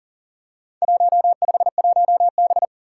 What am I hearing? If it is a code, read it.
Morse code, heard as a keyed tone, about 39 wpm, 700 Hz. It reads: 151B
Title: Four Challenging Letter-Number Combinations 40wpm